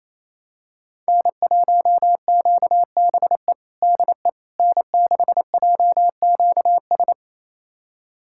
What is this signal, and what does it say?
Morse code, heard as a keyed tone, about 27 wpm, 700 Hz. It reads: N1QBE DE N6JQS
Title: USA Callsign DE Callsign - 5 Characters - 28wpm